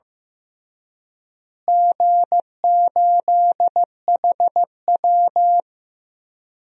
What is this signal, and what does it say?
Morse code, heard as a keyed tone, about 15 wpm, 700 Hz. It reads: G8HW
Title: DXCC-100 Call Signs - 4 Characters - Repeated 2x - 15wpm